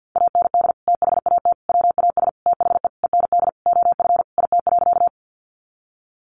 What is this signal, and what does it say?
Morse code, heard as a keyed tone, about 50 wpm, 700 Hz. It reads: AND THAT WAS THE END OF IT.
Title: Sentences from Top 200 Words 50wpm